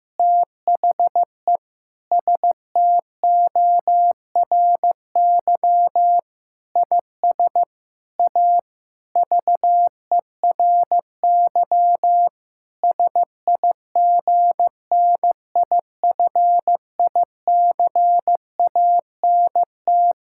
Morse code, heard as a keyed tone, about 15 wpm, 700 Hz. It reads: THE STORY IS A VERY SIGNIFICANT
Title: Sentences from Top 1000 Words - Review Part 6 - 15wpm